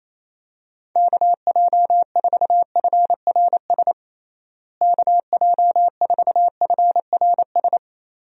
Morse code, heard as a keyed tone, about 28 wpm, 700 Hz. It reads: KJ4FRS KJ4FRS
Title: USA Call Signs - 6 Characters - Repeated 3x - 28wpm